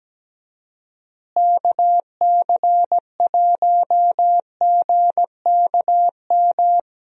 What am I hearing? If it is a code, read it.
Morse code, heard as a keyed tone, about 17 wpm, 700 Hz. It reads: KC1GKM
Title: USA Call Signs - 6 Characters - Repeated 2x - 17wpm